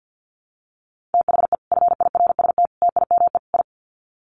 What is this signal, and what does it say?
Morse code, heard as a keyed tone, about 50 wpm, 700 Hz. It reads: THE FIRST TIME I
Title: Sets of 4 Words 50wpm